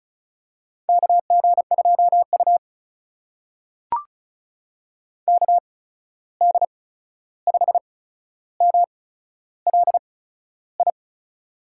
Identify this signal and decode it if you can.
Morse code, heard as a keyed tone, about 34 wpm, spaced as for 13 wpm, 700 Hz. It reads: KG2U KD5MLI
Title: USA Call Signs - 8x Character Spacing - 35wpm